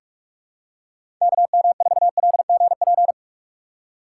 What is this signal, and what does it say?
Morse code, heard as a keyed tone, about 45 wpm, 700 Hz. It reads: KM4LGP
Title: USA Call Signs - 2x Character Spacing - 45wpm